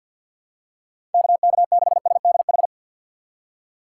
Morse code, heard as a keyed tone, about 52 wpm, 700 Hz. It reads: KK6SDH
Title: USA Call Signs - 6 Characters - 3x Character Spacing - 50wpm